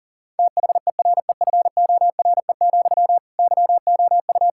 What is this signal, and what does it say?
Morse code, heard as a keyed tone, about 40 wpm, 690 Hz. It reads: THEREFORE, YOU
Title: Sentences from Top 1000 Words - Review Part 5 - 40wpm